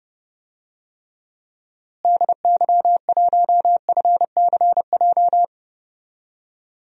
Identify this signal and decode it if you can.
Morse code, heard as a keyed tone, about 30 wpm, 700 Hz. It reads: DY1FCJ
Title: DXCC-100 Call Signs - 6 Characters - Repeated 3x - 30wpm